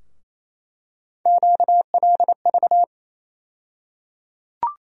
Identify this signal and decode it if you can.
Morse code, heard as a keyed tone, about 28 wpm, 700 Hz. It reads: QLV
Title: USA Call Sign Postfixes - 3 Letters 28wpm